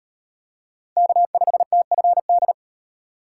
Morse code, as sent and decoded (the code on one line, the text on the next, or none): -.- ..... - ..-. -..
K5TFD